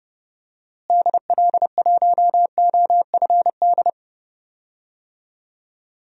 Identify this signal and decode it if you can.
Morse code, heard as a keyed tone, about 30 wpm, 700 Hz. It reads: DL1OFD
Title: DXCC-100 Call Signs - 6 Characters - Repeated 3x - 30wpm